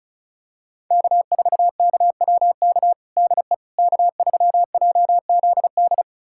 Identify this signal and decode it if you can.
Morse code, heard as a keyed tone, about 36 wpm, 700 Hz. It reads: K4KWK DE K3JZD
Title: USA Callsign DE Callsign - 5 Characters - 35wpm